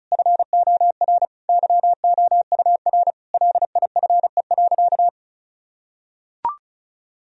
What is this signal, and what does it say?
Morse code, heard as a keyed tone, about 36 wpm, 700 Hz. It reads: FOR YOUR LIFE.
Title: Sentences from Top 1000 Words - Review Part 2 - 35wpm